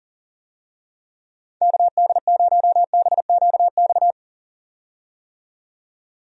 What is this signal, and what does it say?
Morse code, heard as a keyed tone, about 40 wpm, 700 Hz. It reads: KD0BQX
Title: USA Call Signs - 6 Characters - 40wpm